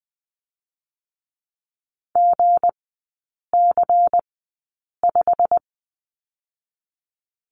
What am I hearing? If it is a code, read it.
Morse code, heard as a keyed tone, about 20 wpm, spaced as for 8 wpm, 700 Hz. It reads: GC5
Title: 218 Lesson 35e - Number 0 - Three Characters 20wpm